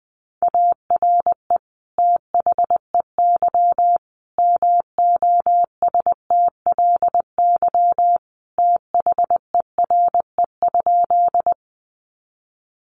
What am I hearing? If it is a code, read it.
Morse code, heard as a keyed tone, about 20 wpm, 700 Hz. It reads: ARE THEY MOSTLY THERE?
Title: Sentences from Top 1300 Words 20wpm